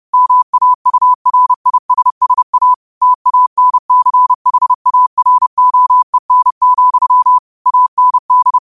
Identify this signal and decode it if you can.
Morse code, heard as a keyed tone, about 30 wpm, 1000 Hz. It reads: MAURISSA TANCHAROEN, AND
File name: qotdfiles / 30wpm